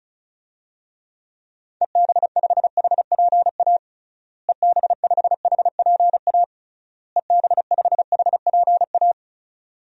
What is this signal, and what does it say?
Morse code, heard as a keyed tone, about 36 wpm, 700 Hz. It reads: EB5HPA EB5HPA EB5HPA
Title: DXCC-100 Call Signs - 6 Characters - Repeated 3x - 35wpm